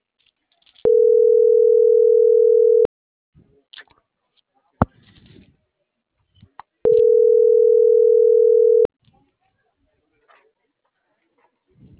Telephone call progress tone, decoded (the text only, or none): ringback tone